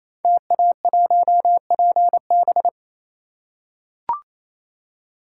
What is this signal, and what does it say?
Morse code, heard as a keyed tone, about 27 wpm, 700 Hz. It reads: TA1PB
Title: Code Groups - 5 Letter:Numbers - 28wpm